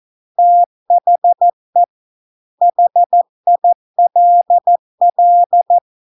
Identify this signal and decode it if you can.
Morse code, heard as a keyed tone, about 14 wpm, 700 Hz. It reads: THE HILL